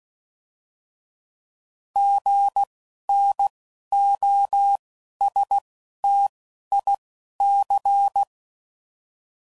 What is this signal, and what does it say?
Morse code, heard as a keyed tone, about 16 wpm, spaced as for 11 wpm, 800 Hz. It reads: GNOSTIC
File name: Prefixes / By First Letter